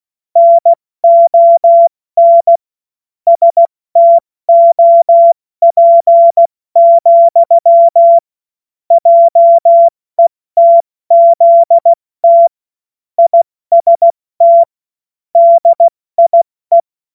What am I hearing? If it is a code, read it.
Morse code, heard as a keyed tone, about 16 wpm, 680 Hz. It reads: NON STOP, JETZT IST DIE
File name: Morsekurs / Witze / Witz 02 April 2024 70 F80